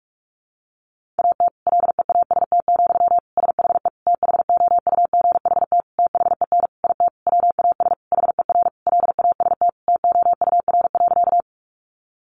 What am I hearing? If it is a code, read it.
Morse code, heard as a keyed tone, about 44 wpm, 700 Hz. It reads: AT LEAST, SHE THOUGHT THEN IT WAS HER LAST TOUR.
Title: Sentences from Top 1400 Words 45wpm